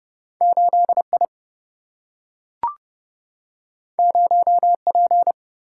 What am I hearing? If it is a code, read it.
Morse code, heard as a keyed tone, about 30 wpm, 700 Hz. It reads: 8 I 0 P
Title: Single-Letter Single-Number 30wpm